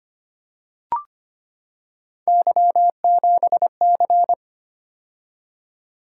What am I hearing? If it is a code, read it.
Morse code, heard as a keyed tone, about 25 wpm, 700 Hz. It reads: Y7C
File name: Three Letter-Number Combinations - 25wpm